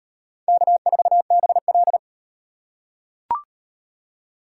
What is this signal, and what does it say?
Morse code, heard as a keyed tone, about 38 wpm, 700 Hz. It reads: K4BL